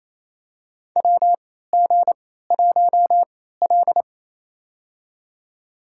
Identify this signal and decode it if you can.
Morse code, heard as a keyed tone, about 28 wpm, spaced as for 16 wpm, 700 Hz. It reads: WG1L
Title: USA Call Signs - 4 Characters - 3x Character Spacing - 28wpm